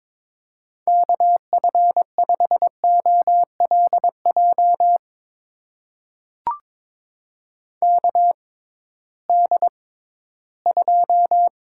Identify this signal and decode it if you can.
Morse code, heard as a keyed tone, about 22 wpm, spaced as for 14 wpm, 700 Hz. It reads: KF5OLJ KD2
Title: USA Call Signs - 6 Characters - 6x Character Spacing - 22wpm